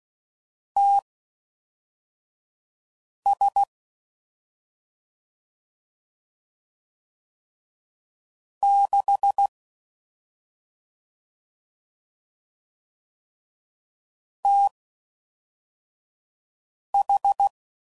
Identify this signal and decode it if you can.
Morse code, heard as a keyed tone, about 16 wpm, spaced as for 4 wpm, 800 Hz. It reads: TS 6 TH
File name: Mixed / CW Comparisons / CW Comparisons Broken Characters